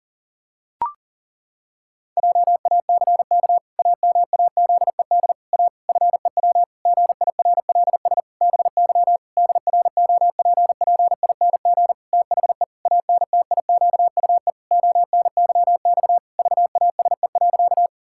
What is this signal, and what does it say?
Morse code, heard as a keyed tone, about 40 wpm, 700 Hz. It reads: JACK AMAZED A FEW GIRLS BY DROPPING THE ANTIQUE ONYX VASE.
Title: Pangram Challenge v2 - 40wpm